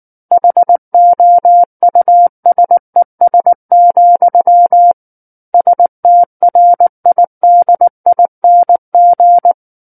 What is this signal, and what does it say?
Morse code, heard as a keyed tone, about 19 wpm, 700 Hz. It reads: HOUSES, STRIDING
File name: Book One / 10-Chapter 10 - 19 WPM